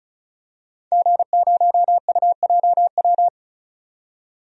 Morse code, heard as a keyed tone, about 34 wpm, 700 Hz. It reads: G0UJW
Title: DXCC-100 Call Signs - 5 Characters - 35wpm